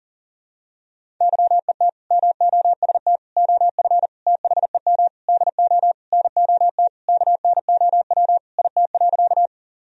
Morse code, heard as a keyed tone, about 40 wpm, 700 Hz. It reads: YET MOST OF THEM DO NOT KNOW IT.
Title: Sentences from Top 1000 Words - Review Part 1 - 40wpm